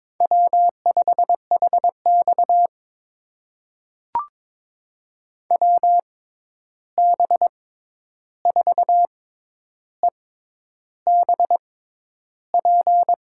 Morse code, heard as a keyed tone, about 22 wpm, spaced as for 10 wpm, 700 Hz. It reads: W5HX WB4EBP
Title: USA Call Signs - 6x Character Spacing - 22wpm